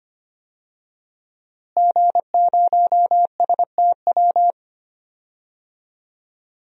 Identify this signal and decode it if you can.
Morse code, heard as a keyed tone, about 25 wpm, 700 Hz. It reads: G0STW
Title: DXCC-100 Call Signs - 5 Characters - Repeated 3x - 25wpm